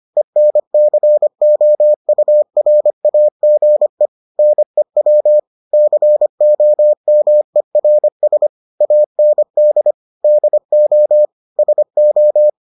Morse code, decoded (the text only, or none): ENCOURAGE NEWCOMERS AND DO SO